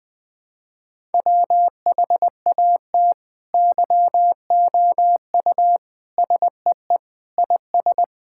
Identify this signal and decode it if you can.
Morse code, heard as a keyed tone, about 20 wpm, 700 Hz. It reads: WHAT YOU SEE IS
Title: Sentences with 7-Word Phrases - 20wpm